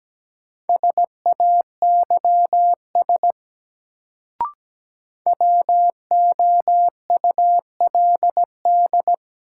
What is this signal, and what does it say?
Morse code, heard as a keyed tone, about 17 wpm, 700 Hz. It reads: SAYS WOULD
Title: Sets of 2 Words - New Word Drill - 17wpm